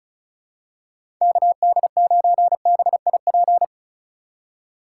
Morse code, heard as a keyed tone, about 36 wpm, 700 Hz. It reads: KD9BIP
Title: USA Call Signs - 6 Characters - 3x Character Spacing - 35wpm